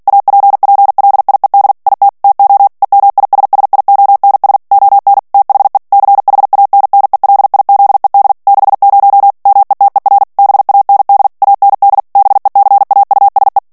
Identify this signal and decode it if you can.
Morse code, heard as a keyed tone, about 48 wpm, 800 Hz. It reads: APPLIED IT TO EMISSIONS ON THE CHANNELIZED 60 METER BAND AND BECAUSE
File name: ANGLAIS texte clair 3 vitesse 45 mots minute